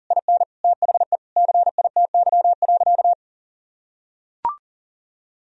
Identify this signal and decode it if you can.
Morse code, heard as a keyed tone, about 37 wpm, 700 Hz. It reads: IN THE CITY.
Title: Sentences from Top 1000 Words - Review Part 3 - 40wpm